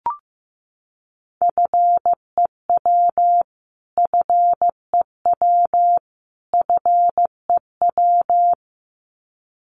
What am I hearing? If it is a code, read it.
Morse code, heard as a keyed tone, about 15 wpm, 700 Hz. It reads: FEW FEW FEW
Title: Top 500 Words Repeated 15wpm